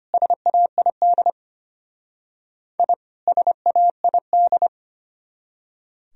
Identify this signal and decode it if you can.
Morse code, heard as a keyed tone, about 33 wpm, 700 Hz. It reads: SAID I SAID